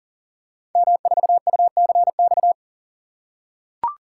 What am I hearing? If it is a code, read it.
Morse code, heard as a keyed tone, about 39 wpm, 700 Hz. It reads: M4UCX
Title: Code Groups - 5 Letter:Numbers - 40wpm